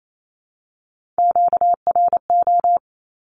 Morse code, heard as a keyed tone, about 28 wpm, 700 Hz. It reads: QRO
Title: Q-Codes 28wpm